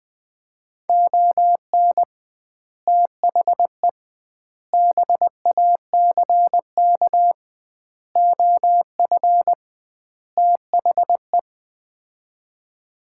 Morse code, heard as a keyed tone, about 21 wpm, 700 Hz. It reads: ON THE BACK OF THE
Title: 2x Word-Spacing - Sets of 5 Words - 20wpm